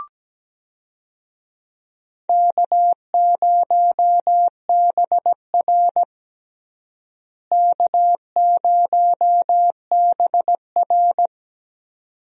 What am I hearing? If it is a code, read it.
Morse code, heard as a keyed tone, about 17 wpm, 700 Hz. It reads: K0BR K0BR
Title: USA Call Signs - 4 Characters - Repeated 2x - 17wpm